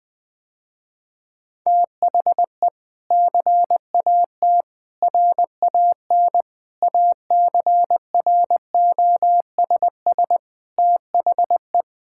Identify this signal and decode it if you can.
Morse code, heard as a keyed tone, about 20 wpm, 700 Hz. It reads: THE CAT RAN ACROSS THE
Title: Sentences with 2-Word Phrases - Spoken Beginning - 20wpm